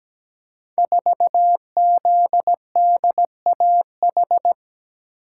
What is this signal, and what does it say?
Morse code, heard as a keyed tone, about 17 wpm, 700 Hz. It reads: 4ZDAH
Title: Code Groups - 5 Letter:Numbers - 17wpm